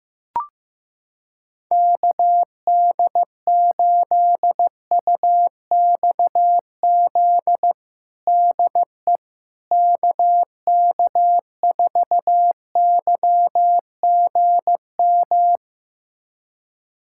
Morse code, heard as a keyed tone, about 15 wpm, 700 Hz. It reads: KD8UXZ DE KK4YGM
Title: USA Callsign DE Callsign - 6 Characters - 15wpm